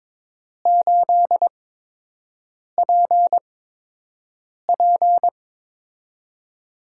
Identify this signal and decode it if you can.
Morse code, heard as a keyed tone, about 22 wpm, spaced as for 6 wpm, 700 Hz. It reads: 8PP
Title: USA Call Signs - 4 Characters - 8x Character Spacing - 22wpm